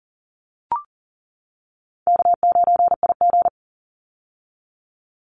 Morse code, heard as a keyed tone, about 40 wpm, 700 Hz. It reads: K9IG
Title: Call Signs - 40wpm